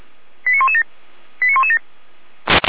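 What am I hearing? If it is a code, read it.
Five-tone selective call (ZVEI), heard as quick runs of five tones, tones of about 70 ms. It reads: 89117 89117
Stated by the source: ZVEI five-tone selective call 89085 received via SDR.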